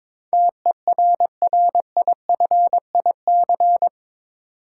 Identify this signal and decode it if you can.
Morse code, heard as a keyed tone, about 22 wpm, 700 Hz. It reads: TERRIFIC
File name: practice / 22wpm / Eight Letter Words - 22wpm